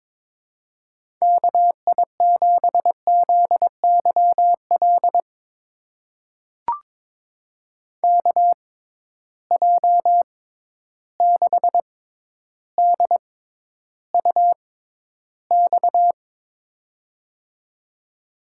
Morse code, heard as a keyed tone, about 21 wpm, spaced as for 11 wpm, 700 Hz. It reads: KI7ZYL KJ6DUX
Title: USA Call Signs - 6x Character Spacing - 22wpm